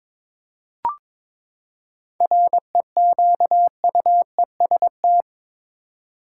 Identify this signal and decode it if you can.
Morse code, heard as a keyed tone, about 21 wpm, 700 Hz. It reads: REQUEST
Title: Seven Letter Words - 22wpm